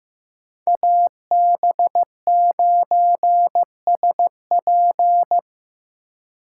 Morse code, heard as a keyed tone, about 17 wpm, 700 Hz. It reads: AB9SP
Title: Speed Racing - USA 2x2 Call Signs 15wpm